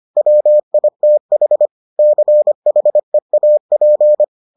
Morse code, heard as a keyed tone, about 25 wpm, 600 Hz. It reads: WITH CHEAP